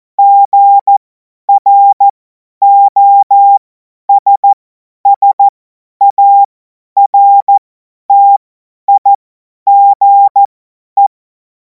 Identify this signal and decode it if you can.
Morse code, heard as a keyed tone, about 14 wpm, spaced as for 10 wpm, 800 Hz. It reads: GROSSARTIGE
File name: Morsekurs / Witze / Witz 12. Mai 2026 50 F70